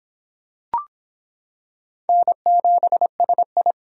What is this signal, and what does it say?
Morse code, heard as a keyed tone, about 25 wpm, 700 Hz. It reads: N7SI